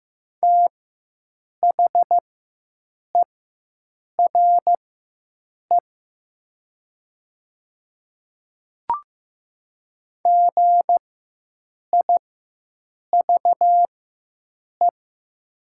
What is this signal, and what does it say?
Morse code, heard as a keyed tone, about 15 wpm, spaced as for 7 wpm, 700 Hz. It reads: THERE GIVE
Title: 4x Character Spacing - Top 200 Words - 15wpm